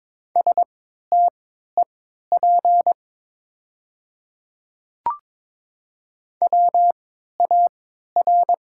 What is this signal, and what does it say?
Morse code, heard as a keyed tone, about 22 wpm, spaced as for 12 wpm, 700 Hz. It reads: STEP WAR
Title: 3x Character Spacing - Top 500 Words - 22wpm